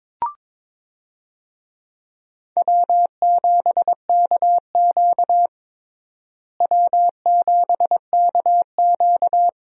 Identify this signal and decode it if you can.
Morse code, heard as a keyed tone, about 23 wpm, 700 Hz. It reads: W7KQ W7KQ
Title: USA Call Signs - 4 Characters - Repeated 3x - 22wpm